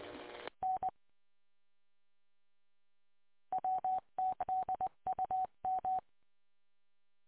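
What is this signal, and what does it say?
Morse code, heard as a keyed tone, about 20 wpm, 750 Hz. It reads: N WTDUM
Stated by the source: Morse code from the W8UM audio archive, 145.230.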